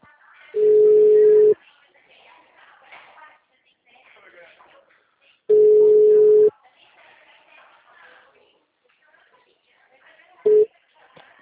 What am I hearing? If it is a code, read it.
Ringback tone (Europe, 425 Hz), with speech.
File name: out- -2012- - - .658